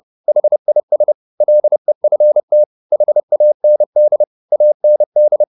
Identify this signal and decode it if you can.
Morse code, heard as a keyed tone, about 30 wpm, 600 Hz. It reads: HIS LEFT HAND AND
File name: the old man and the sea / the old man and the sea 1 30wpm